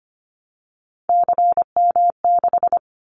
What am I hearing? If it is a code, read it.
Morse code, heard as a keyed tone, about 25 wpm, 700 Hz. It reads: CM6
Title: Callsign Prefixes for Top 100 DXCC - 25wpm